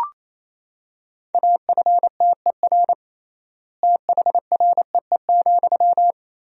Morse code, heard as a keyed tone, about 28 wpm, 700 Hz. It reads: AFTER THREE,
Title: 3x Word-Spacing - Sentences from Top 200 Words - 28wpm